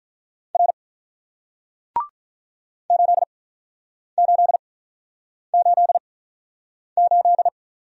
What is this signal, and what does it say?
Morse code, heard as a keyed tone, about 47 wpm, 700 Hz. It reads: R 8 8 8 8